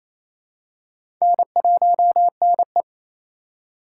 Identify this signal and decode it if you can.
Morse code, heard as a keyed tone, about 28 wpm, 700 Hz. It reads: N1NE
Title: USA Call Signs - 4x Character Spacing - 28wpm